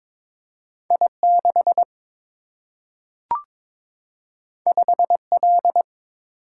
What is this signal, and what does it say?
Morse code, heard as a keyed tone, about 22 wpm, 700 Hz. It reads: I6 5L
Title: Single-Letter Single-Number 22wpm